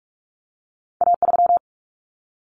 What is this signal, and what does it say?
Morse code, heard as a keyed tone, about 46 wpm, 700 Hz. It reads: A3
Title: Common Cut Numbers - 2-Digit 45wpm